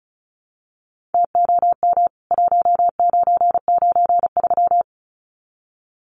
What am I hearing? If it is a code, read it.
Morse code, heard as a keyed tone, about 34 wpm, 700 Hz. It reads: TOM 1993
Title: CWT Exchange 35wpm